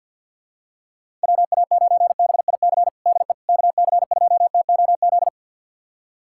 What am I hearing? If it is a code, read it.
Morse code, heard as a keyed tone, about 50 wpm, 700 Hz. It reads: WA9BIC DE KC2TOZ
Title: USA Callsign DE Callsign 50wpm